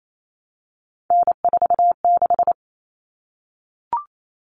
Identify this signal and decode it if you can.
Morse code, heard as a keyed tone, about 28 wpm, 700 Hz. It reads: N46
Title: Common Cut Numbers - 3-Digit 28wpm